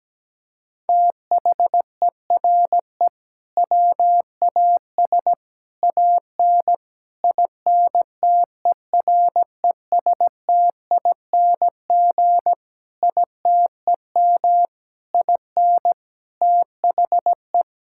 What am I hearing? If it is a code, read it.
Morse code, heard as a keyed tone, about 17 wpm, 700 Hz. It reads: THERE WAS AN INTERESTING ITEM IN THE
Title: Sentences from Top 1000 Words - Review Part 6 - 17wpm